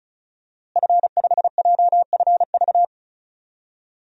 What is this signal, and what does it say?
Morse code, heard as a keyed tone, about 36 wpm, 700 Hz. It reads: F5JFV
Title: DXCC-100 Call Signs - 5 Characters - Repeated 2x - 35wpm